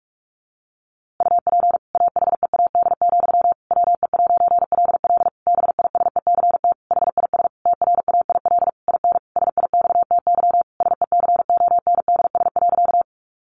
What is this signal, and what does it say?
Morse code, heard as a keyed tone, about 44 wpm, 700 Hz. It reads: UP AHEAD, WE'LL BISECT HIS TRAIL IN SIXTY SECONDS.
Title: A Sound Of Thunder 45wpm